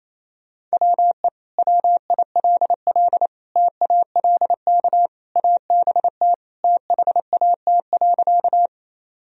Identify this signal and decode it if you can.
Morse code, heard as a keyed tone, about 28 wpm, 700 Hz. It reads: WE WILL TALK ABT THAT.
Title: Sentences from Top 400 Words and Abbreviations 28wpm